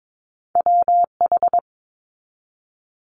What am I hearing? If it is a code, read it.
Morse code, heard as a keyed tone, about 22 wpm, 700 Hz. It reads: WH
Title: Multi-letter Phonemes 22wpm